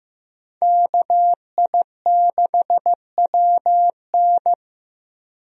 Morse code, heard as a keyed tone, about 15 wpm, 700 Hz. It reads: KI6WN
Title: USA Call Signs - 5 Characters - Repeated 2x - 15wpm